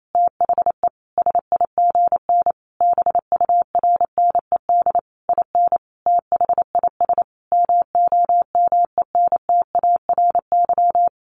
Morse code, heard as a keyed tone, about 28 wpm, 700 Hz. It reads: THE SIGN BURNED IN THIS MOMENTARY
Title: A Sound Of Thunder 28wpm